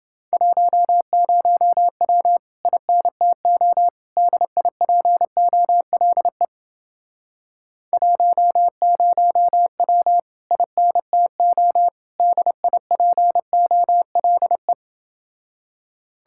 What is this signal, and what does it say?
Morse code, heard as a keyed tone, about 30 wpm, 700 Hz. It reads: 10W INTO DIPOLE 10W INTO DIPOLE